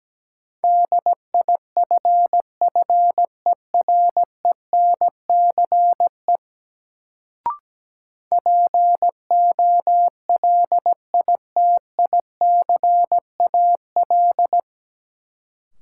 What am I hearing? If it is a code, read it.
Morse code, heard as a keyed tone, about 17 wpm, 700 Hz. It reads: DIFFERENCE POLITICAL
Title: Sentences from Top 800 Words - New Word Drill - 17wpm